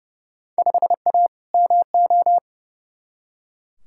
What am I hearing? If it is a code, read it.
Morse code, heard as a keyed tone, about 30 wpm, 700 Hz. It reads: 5A MO
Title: Field Day Exchanges Encore - 30wpm